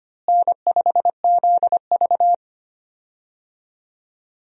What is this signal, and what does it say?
Morse code, heard as a keyed tone, about 25 wpm, 700 Hz. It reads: N5ZV
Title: USA Call Signs - 4 Characters - Repeated 3x - 25wpm